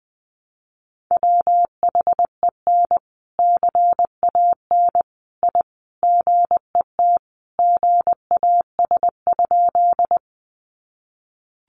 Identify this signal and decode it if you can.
Morse code, heard as a keyed tone, about 19 wpm, 700 Hz. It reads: WHEN CAN I GET GAS?
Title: Sentences from Top 1100 Words 20wpm